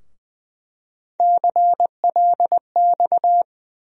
Morse code, heard as a keyed tone, about 20 wpm, 700 Hz. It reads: CLX
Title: USA Call Sign Postfixes - 3 Letters 20wpm